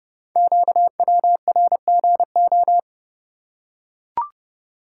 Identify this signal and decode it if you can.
Morse code, heard as a keyed tone, about 30 wpm, 700 Hz. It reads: QWRGO
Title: Code Groups - 5 Letters - 30wpm